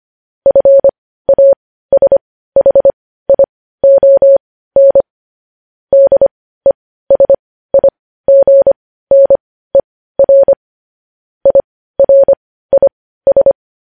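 Morse code, heard as a keyed tone, about 25 wpm, spaced as for 15 wpm, 550 Hz. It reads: FASHION DESIGNER IRIS
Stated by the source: News headlines sent in Morse code at 15 words per minute, preceded by vvv.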